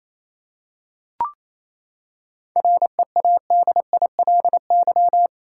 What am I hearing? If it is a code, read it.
Morse code, heard as a keyed tone, about 28 wpm, 700 Hz. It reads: READILY
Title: Seven Letter Words - 28wpm